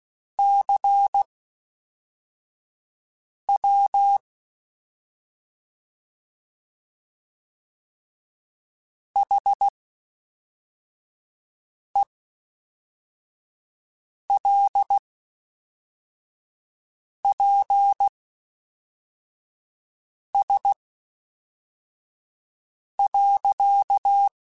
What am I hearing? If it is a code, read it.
Morse code, heard as a keyed tone, about 16 wpm, spaced as for 4 wpm, 800 Hz. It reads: CW HELPS.
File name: Misc / Misc-mp3s / QST QST @5